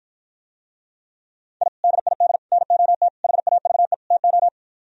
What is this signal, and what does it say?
Morse code, heard as a keyed tone, about 52 wpm, 700 Hz. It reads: I DID NOT HAVE TO